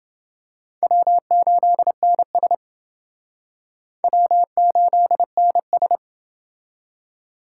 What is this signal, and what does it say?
Morse code, heard as a keyed tone, about 28 wpm, 700 Hz. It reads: W8NS W8NS